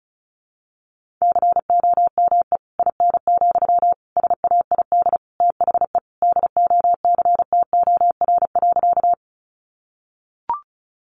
Morse code, heard as a keyed tone, about 34 wpm, 700 Hz. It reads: COME IN, SAID THE DOCTOR.
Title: Sentences from Top 1500 Words 35wpm